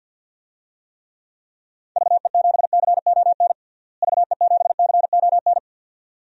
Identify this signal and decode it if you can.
Morse code, heard as a keyed tone, about 47 wpm, 700 Hz. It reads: VE7CON VE7CON
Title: DXCC-100 Call Signs - 6 Characters - Repeated 2x - 50wpm